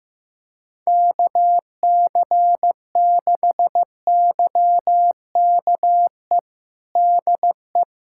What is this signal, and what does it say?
Morse code, heard as a keyed tone, about 15 wpm, 700 Hz. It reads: KC6YKE DE
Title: USA Callsign DE Callsign 15wpm